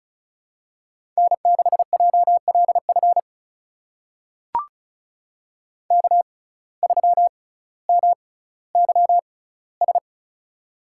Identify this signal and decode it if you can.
Morse code, heard as a keyed tone, about 36 wpm, spaced as for 18 wpm, 700 Hz. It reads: N6JLF K3MYS
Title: USA Call Signs - 5 Characters - 6x Character Spacing - 35wpm